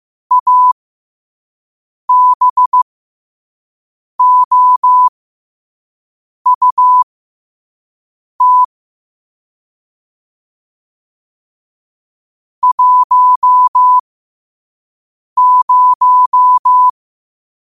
Morse code, heard as a keyed tone, about 15 wpm, spaced as for 5 wpm, 1000 Hz. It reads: ABOUT 10
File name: qotdfiles / 5wpm